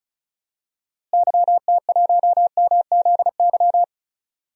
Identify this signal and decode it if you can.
Morse code, heard as a keyed tone, about 36 wpm, 700 Hz. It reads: YT1MZY
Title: DXCC-100 Call Signs - 6 Characters - Repeated 2x - 35wpm